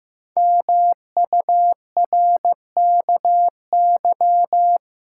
Morse code, heard as a keyed tone, about 15 wpm, 700 Hz. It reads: MURKY
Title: Pangram Challenge v2 - 15wpm